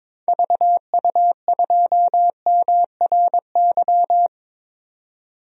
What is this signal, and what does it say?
Morse code, heard as a keyed tone, about 22 wpm, 700 Hz. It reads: VU2MRY
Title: DXCC-100 Call Signs - 6 Characters - 22wpm